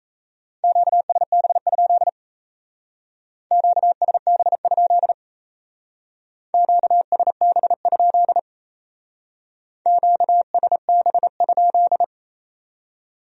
Morse code, heard as a keyed tone, about 33 wpm, 700 Hz. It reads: QSB? QSB? QSB? QSB?
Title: Speed-Racing Q-Codes - 28wpm